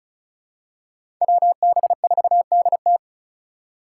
Morse code, heard as a keyed tone, about 36 wpm, 700 Hz. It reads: WB4DT
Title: USA Call Signs - 8x Character Spacing - 35wpm